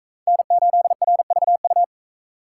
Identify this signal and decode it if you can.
Morse code, heard as a keyed tone, about 38 wpm, 700 Hz. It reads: N8RVU